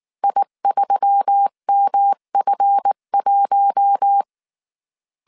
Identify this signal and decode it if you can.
Morse code, heard as a keyed tone, about 19 wpm, 800 Hz. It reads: I3MF1
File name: CW / DET / 1 9 1500C S C 18wpm 2539